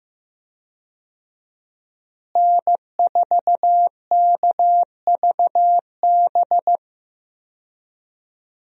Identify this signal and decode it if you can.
Morse code, heard as a keyed tone, about 15 wpm, 700 Hz. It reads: N4KVB
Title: USA Call Signs - 5 Characters - Repeated 3x - 15wpm